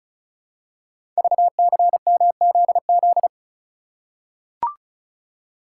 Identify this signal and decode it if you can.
Morse code, heard as a keyed tone, about 34 wpm, 700 Hz. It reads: VCMZZ
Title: Code Groups - 5 Letters - 35wpm